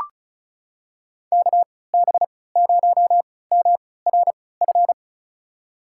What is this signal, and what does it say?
Morse code, heard as a keyed tone, about 34 wpm, spaced as for 20 wpm, 700 Hz. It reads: KB0MRF
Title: USA Call Signs - 3x Character Spacing - 35wpm